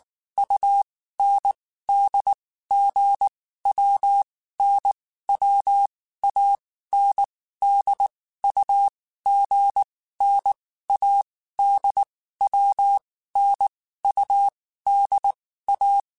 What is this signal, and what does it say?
Morse code, heard as a keyed tone, about 19 wpm, 800 Hz. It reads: U N D G W N W A N D U G N A D W N U D A
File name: Groups@18 / Groups@18 mp3 / AWUNDG@18